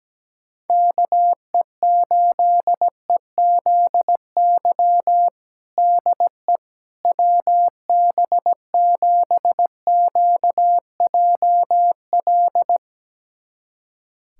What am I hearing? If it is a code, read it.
Morse code, heard as a keyed tone, about 17 wpm, 700 Hz. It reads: KE8EZY DE WB7QJL
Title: USA Callsign DE Callsign - 6 Characters - 17wpm